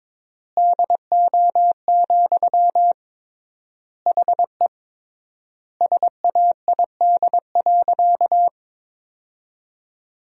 Morse code, heard as a keyed tone, about 22 wpm, 700 Hz. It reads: DO, HE SAID.
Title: 3x Word-Spacing - Sentences from Top 100 Words - 22wpm